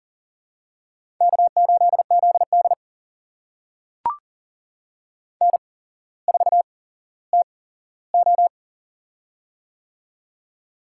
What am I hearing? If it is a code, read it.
Morse code, heard as a keyed tone, about 40 wpm, spaced as for 17 wpm, 700 Hz. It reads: K8ZD N4TO
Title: USA Call Signs - 4 Characters - 8x Character Spacing - 40wpm